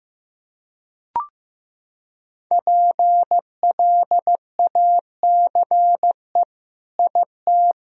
Morse code, heard as a keyed tone, about 15 wpm, 700 Hz. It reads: PLACE IT
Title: Sentences from Top 1000 Words - Review Part 2 - 15wpm